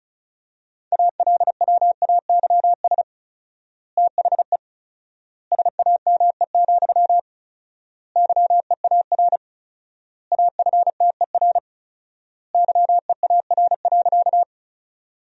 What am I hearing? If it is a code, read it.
Morse code, heard as a keyed tone, about 36 wpm, 700 Hz. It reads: ALWAYS THE SAME, YEAR AFTER YEAR.
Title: 4x Word-Spacing - Sentences from Top 300 Words - 35wpm